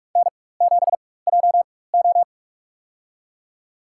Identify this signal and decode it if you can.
Morse code, heard as a keyed tone, about 46 wpm, spaced as for 21 wpm, 700 Hz. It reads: N7JO
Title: USA Call Signs - 4 Characters - 4x Character Spacing - 45wpm